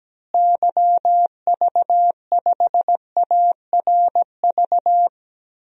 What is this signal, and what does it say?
Morse code, heard as a keyed tone, about 17 wpm, 700 Hz. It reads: YV5ARV
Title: DXCC-100 Call Signs - 6 Characters - Repeated 3x - 17wpm